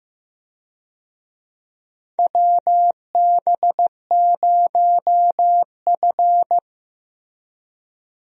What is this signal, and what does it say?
Morse code, heard as a keyed tone, about 16 wpm, 700 Hz. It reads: WB0F
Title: USA Call Signs - 4 Characters - Repeated 2x - 15wpm